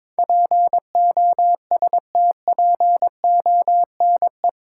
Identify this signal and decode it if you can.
Morse code, heard as a keyed tone, about 22 wpm, 700 Hz. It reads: POSTPONE
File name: practice / 22wpm / Eight Letter Words - 22wpm